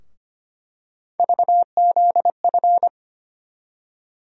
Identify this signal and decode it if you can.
Morse code, heard as a keyed tone, about 25 wpm, 700 Hz. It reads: VZF
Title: USA Call Sign Postfixes - 3 Letters 25wpm